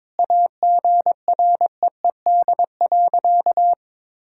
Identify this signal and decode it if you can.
Morse code, heard as a keyed tone, about 22 wpm, 700 Hz. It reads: AGREED.
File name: practice / 22wpm / Sentences with 3-Word Phrases - 22wpm